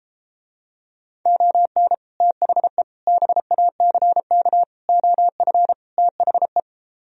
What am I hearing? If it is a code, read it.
Morse code, heard as a keyed tone, about 32 wpm, 700 Hz. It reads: ON THE BACK OF THE